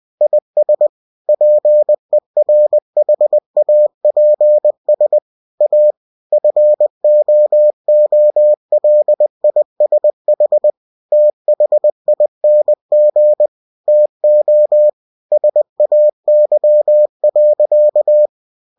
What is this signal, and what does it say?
Morse code, heard as a keyed tone, about 20 wpm, 600 Hz. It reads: IS PERHAPS A FOOLISH THING TO SAY.
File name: cw-books / winnie-the-pooh / winnie-20wpm-page-0062